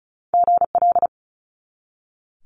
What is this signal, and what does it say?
Morse code, heard as a keyed tone, about 36 wpm, 700 Hz. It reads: GL
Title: Multi-letter Phonemes 35wpm